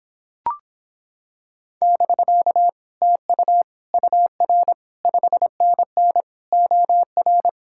Morse code, heard as a keyed tone, about 26 wpm, 700 Hz. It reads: <BK> TU UR 5NN OR